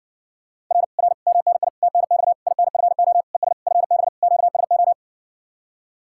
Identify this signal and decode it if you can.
Morse code, heard as a keyed tone, about 60 wpm, 700 Hz. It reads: R R MNI TNX INFO ES FB QSO